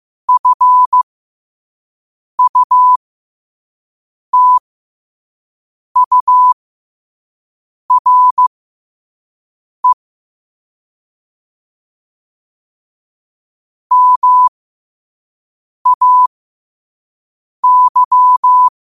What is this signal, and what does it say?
Morse code, heard as a keyed tone, about 15 wpm, spaced as for 5 wpm, 1000 Hz. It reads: FUTURE MAY
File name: qotdfiles / 5wpm